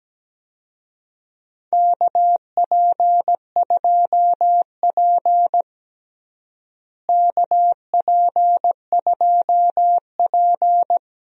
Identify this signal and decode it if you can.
Morse code, heard as a keyed tone, about 17 wpm, 700 Hz. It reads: KP2P KP2P
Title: DXCC-100 Call Signs - 4 Characters - Repeated 2x - 17wpm